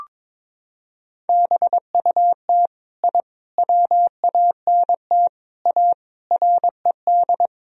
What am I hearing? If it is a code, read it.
Morse code, heard as a keyed tone, about 22 wpm, 700 Hz. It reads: BUT I WANT A RED
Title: Sentences from Top 1000 Words - Review Part 2 - 22wpm